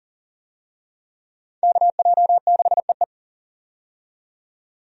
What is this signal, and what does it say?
Morse code, heard as a keyed tone, about 40 wpm, 700 Hz. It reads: KJ6EE
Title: USA Call Signs - 5 Characters - 40wpm